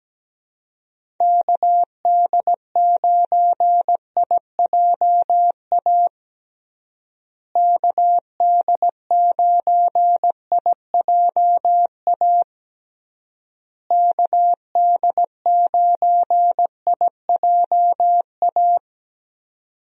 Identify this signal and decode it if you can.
Morse code, heard as a keyed tone, about 17 wpm, 700 Hz. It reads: KD9IJA KD9IJA KD9IJA
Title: USA Call Signs - 6 Characters - Repeated 3x - 17wpm